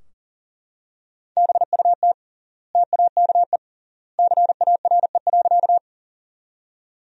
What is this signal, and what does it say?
Morse code, heard as a keyed tone, about 40 wpm, 700 Hz. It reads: BUT TAKE CARE.
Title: 3x Word-Spacing - Sentences from Top 300 Words - 40wpm